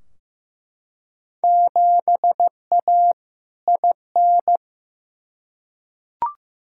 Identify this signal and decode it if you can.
Morse code, heard as a keyed tone, about 15 wpm, 700 Hz. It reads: 7A IN
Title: Field Day Exchanges Encore - 15wpm